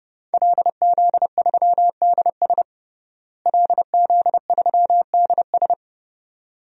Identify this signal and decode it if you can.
Morse code, heard as a keyed tone, about 30 wpm, 700 Hz. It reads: LZ3DS LZ3DS
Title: DXCC-100 Call Signs - 5 Characters - Repeated 2x - 30wpm